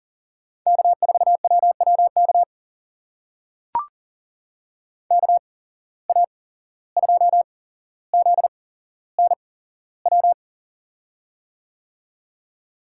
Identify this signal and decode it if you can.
Morse code, heard as a keyed tone, about 40 wpm, spaced as for 16 wpm, 700 Hz. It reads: K4WWK KA2ZNW
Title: USA Call Signs - 8x Character Spacing - 40wpm